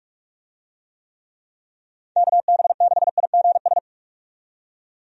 Morse code, heard as a keyed tone, about 47 wpm, 700 Hz. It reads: KB6IGS
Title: USA Call Signs - 6 Characters - 45wpm